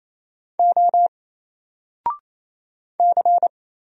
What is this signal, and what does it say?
Morse code, heard as a keyed tone, about 28 wpm, 700 Hz. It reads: O C
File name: Single-Letter-Number-28wpm